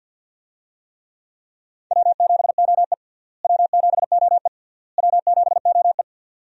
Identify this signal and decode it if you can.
Morse code, heard as a keyed tone, about 48 wpm, 700 Hz. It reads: W7OE W7OE W7OE
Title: USA Call Signs - 4 Characters - Repeated 3x - 50wpm